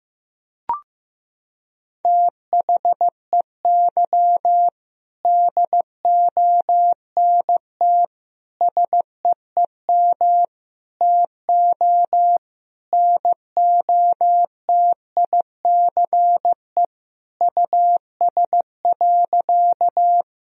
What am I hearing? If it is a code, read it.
Morse code, heard as a keyed tone, about 15 wpm, 700 Hz. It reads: THEY DONT SEEM TO NOTICE US.
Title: Sentences from Top 1000 Words - Review Part 2 - 15wpm